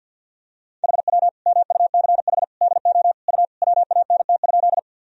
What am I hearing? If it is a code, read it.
Morse code, heard as a keyed tone, about 47 wpm, 700 Hz. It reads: HW MUCH DO U WANT?
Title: Sentences from Top 200 Words and Abbreviations 50wpm